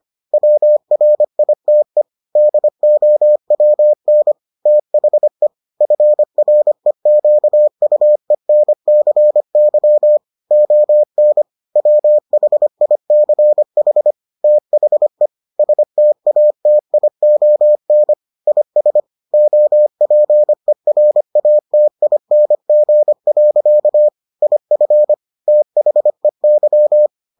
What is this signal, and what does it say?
Morse code, heard as a keyed tone, about 25 wpm, 600 Hz. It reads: WRITE DOWN THE FREQUENCY ON WHICH THE STATION IS OPERATING. IF THEY